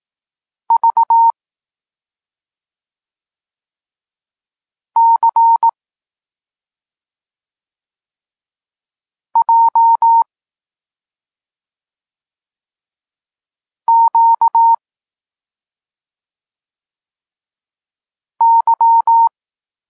Morse code, heard as a keyed tone, about 18 wpm, 920 Hz. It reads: V C J Q Y